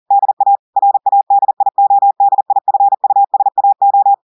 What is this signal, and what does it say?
Morse code, heard as a keyed tone, about 40 wpm, 800 Hz. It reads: DA RADIODIFUSAO
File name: audios / portugais / portugais texte en clair 1 vitesse 40 mots minutes